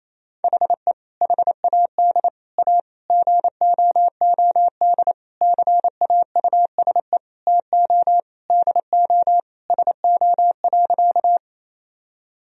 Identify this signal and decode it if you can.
Morse code, heard as a keyed tone, about 28 wpm, 700 Hz. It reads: HE HAD A GOOD CAUSE TO DO SO.
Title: Sentences from Top 1000 Words - Review Part 1 - 28wpm